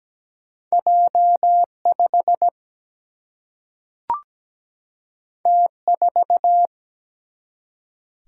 Morse code, heard as a keyed tone, about 17 wpm, 700 Hz. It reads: J5 T4
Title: Single-Letter Single-Number 17wpm